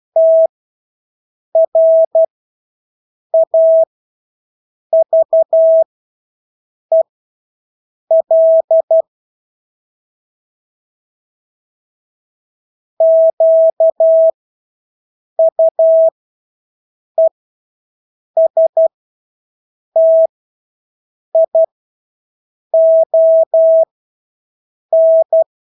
Morse code, heard as a keyed tone, about 12 wpm, spaced as for 6 wpm, 650 Hz. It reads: TRAVEL QUESTION
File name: Engelske ord 12-6 wpm